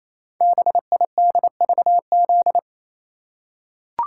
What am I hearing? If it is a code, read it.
Morse code, heard as a keyed tone, about 29 wpm, 700 Hz. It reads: BIDVZ
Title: Code Groups - 5 Letter:Numbers - 28wpm